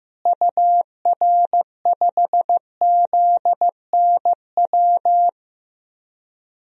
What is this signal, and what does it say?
Morse code, heard as a keyed tone, about 15 wpm, 700 Hz. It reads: UR5ZNW
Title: DXCC-100 Call Signs - 6 Characters - Repeated 3x - 15wpm